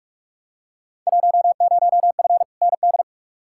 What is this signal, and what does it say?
Morse code, heard as a keyed tone, about 42 wpm, 700 Hz. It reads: 10F ND
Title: Field Day Exchanges Encore - 45wpm